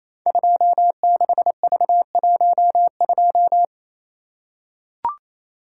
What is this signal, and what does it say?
Morse code, heard as a keyed tone, about 28 wpm, 700 Hz. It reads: 26V12
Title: Code Groups - 5 Letter:Numbers - 28wpm